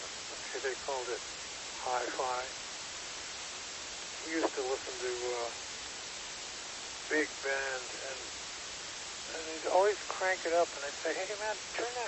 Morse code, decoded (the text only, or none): none